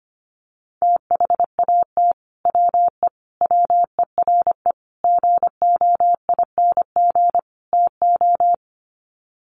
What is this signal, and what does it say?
Morse code, heard as a keyed tone, about 25 wpm, 700 Hz. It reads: THAT WE WERE GOING TO
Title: Sets of 5 Words Encore2 25wpm